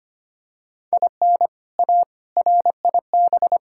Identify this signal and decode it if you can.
Morse code, heard as a keyed tone, about 25 wpm, 700 Hz. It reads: IN A RIB
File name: practice / 25wpm / All-Age Appropriate Jokes - 25wpm